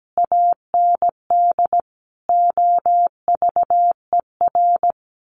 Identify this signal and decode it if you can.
Morse code, heard as a keyed tone, about 17 wpm, 700 Hz. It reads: AND OVER
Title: Sets of 5 Words Encore 17wpm